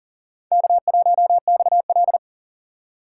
Morse code, heard as a keyed tone, about 40 wpm, 700 Hz. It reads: K1XL
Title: USA Call Signs - 6x Character Spacing - 40wpm